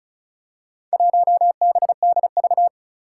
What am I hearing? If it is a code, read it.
Morse code, heard as a keyed tone, about 36 wpm, 700 Hz. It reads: 1BDV
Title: Four Challenging Letter-Number Combinations 35wpm